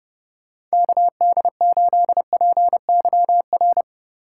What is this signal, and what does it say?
Morse code, heard as a keyed tone, about 29 wpm, 700 Hz. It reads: KD8PYR